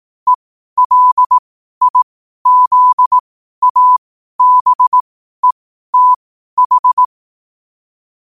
Morse code, heard as a keyed tone, about 18 wpm, spaced as for 12 wpm, 1000 Hz. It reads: ELIZABETH
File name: qotdfiles / 12wpm